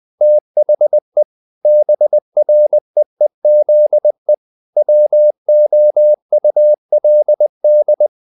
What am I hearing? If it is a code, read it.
Morse code, heard as a keyed tone, about 20 wpm, 600 Hz. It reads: THE BREEZE WOULD
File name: the old man and the sea part 2 20wpm